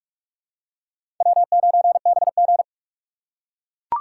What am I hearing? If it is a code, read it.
Morse code, heard as a keyed tone, about 41 wpm, 700 Hz. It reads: W9BG
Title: USA Call Signs - 3x Character Spacing - 45wpm